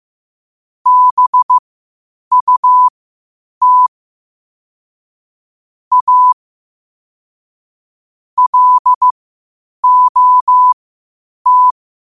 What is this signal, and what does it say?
Morse code, heard as a keyed tone, about 15 wpm, spaced as for 8 wpm, 1000 Hz. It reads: BUT A LOT
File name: qotdfiles / 8wpm